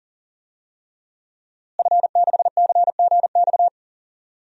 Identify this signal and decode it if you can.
Morse code, heard as a keyed tone, about 40 wpm, 700 Hz. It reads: F6CGX
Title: DXCC-100 Call Signs - 5 Characters - Repeated 3x - 40wpm